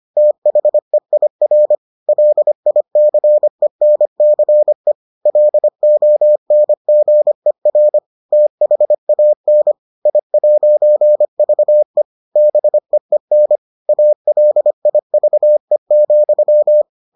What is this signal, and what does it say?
Morse code, heard as a keyed tone, about 25 wpm, 600 Hz. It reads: THEIR LICENCE LONGER THAN I'VE BEEN ALIVE,